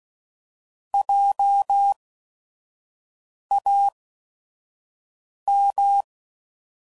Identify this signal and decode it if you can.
Morse code, heard as a keyed tone, about 16 wpm, spaced as for 5 wpm, 800 Hz. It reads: JAM